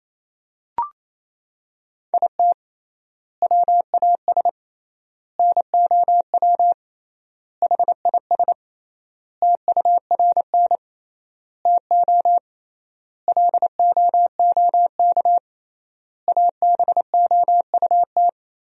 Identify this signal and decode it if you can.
Morse code, heard as a keyed tone, about 28 wpm, 700 Hz. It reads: IT WAS NOW HIS TURN TO LOOK ABOUT
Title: 3x Word-Spacing - Sentences from Top 200 Words - 28wpm